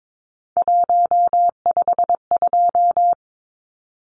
Morse code, heard as a keyed tone, about 22 wpm, 700 Hz. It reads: 152
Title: Numbers - 3-Digit 22wpm